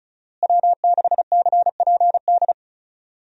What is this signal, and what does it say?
Morse code, heard as a keyed tone, about 36 wpm, 700 Hz. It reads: W6CPD
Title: USA Call Signs - 5 Characters - Repeated 3x - 35wpm